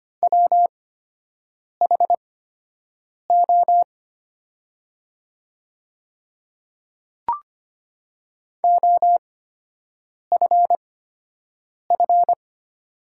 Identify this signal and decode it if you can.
Morse code, heard as a keyed tone, about 25 wpm, spaced as for 7 wpm, 700 Hz. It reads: WHO OFF
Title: 8x Character Spacing - Top 200 Words - 25wpm